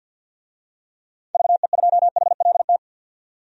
Morse code, heard as a keyed tone, about 50 wpm, 700 Hz. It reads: VE2HLT
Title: DXCC-100 Call Signs - 6 Characters - 50wpm